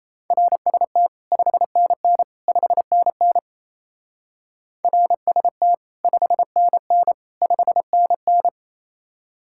Morse code, heard as a keyed tone, about 34 wpm, 700 Hz. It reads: RST 5NN 5NN RST 5NN 5NN
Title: Speed-Racing Common QSO Phrases - 28wpm